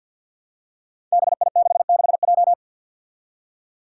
Morse code, heard as a keyed tone, about 53 wpm, 700 Hz. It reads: BI66J
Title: Five Challenging Letter-Number Combinations 50wpm